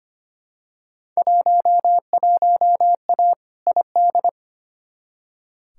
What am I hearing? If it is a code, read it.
Morse code, heard as a keyed tone, about 25 wpm, 700 Hz. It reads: 11A ID
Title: Field Day Exchanges Encore - 25wpm